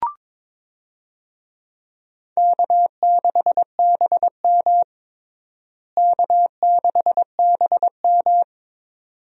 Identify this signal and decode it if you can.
Morse code, heard as a keyed tone, about 22 wpm, 700 Hz. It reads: K6BM K6BM
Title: USA Call Signs - 4 Characters - Repeated 2x - 22wpm